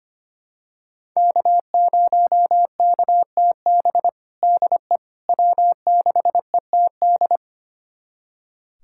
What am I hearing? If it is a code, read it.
Morse code, heard as a keyed tone, about 25 wpm, 700 Hz. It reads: K0KTB DE W6ETD
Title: USA Callsign DE Callsign - 5 Characters - 25wpm